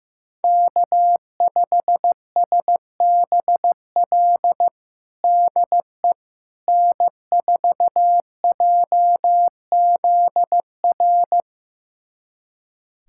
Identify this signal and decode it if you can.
Morse code, heard as a keyed tone, about 15 wpm, 700 Hz. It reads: K5SBL DE N4JZR
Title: USA Callsign DE Callsign - 5 Characters - 15wpm